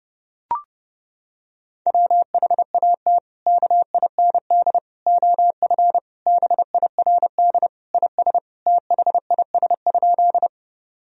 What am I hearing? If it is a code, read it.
Morse code, heard as a keyed tone, about 30 wpm, 700 Hz. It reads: WHAT KIND OF BIRD IS THIS?
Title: Sentences from Top 1000 Words - Review Part 2 - 30wpm